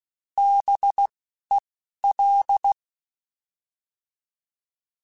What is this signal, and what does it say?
Morse code, heard as a keyed tone, about 16 wpm, spaced as for 11 wpm, 800 Hz. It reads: BEL
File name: Alphabetical / Alphabetical mp3s / Bs @10